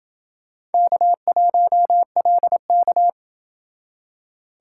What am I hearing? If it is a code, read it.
Morse code, heard as a keyed tone, about 30 wpm, 700 Hz. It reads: K1LK